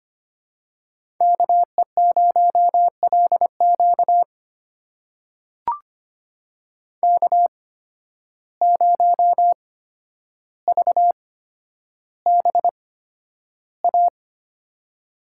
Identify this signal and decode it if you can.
Morse code, heard as a keyed tone, about 25 wpm, spaced as for 11 wpm, 700 Hz. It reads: KE0LQ K0VBA
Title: USA Call Signs - 5 Characters - 8x Character Spacing - 25wpm